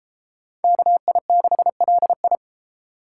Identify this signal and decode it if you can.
Morse code, heard as a keyed tone, about 32 wpm, 700 Hz. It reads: KI6LI